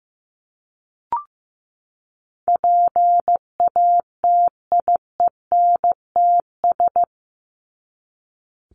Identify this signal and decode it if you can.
Morse code, heard as a keyed tone, about 15 wpm, 700 Hz. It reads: PATIENTS
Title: ES:S Suffix - Top 100 Words with Base Words 15wpm